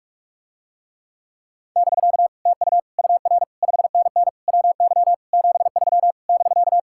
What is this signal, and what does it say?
Morse code, heard as a keyed tone, about 44 wpm, 700 Hz. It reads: <BK> TU UR 5NN WY 73 <BK>